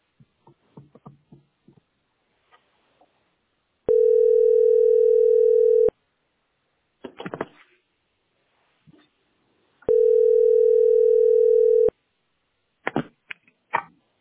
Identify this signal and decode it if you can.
Ringback tone (North America), 440 + 480 Hz.